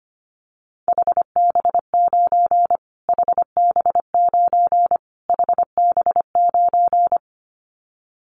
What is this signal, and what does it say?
Morse code, heard as a keyed tone, about 25 wpm, 700 Hz. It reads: HB9 HB9 HB9
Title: Callsign Prefixes for Top 25 DXCC - Repeated 3x - 25wpm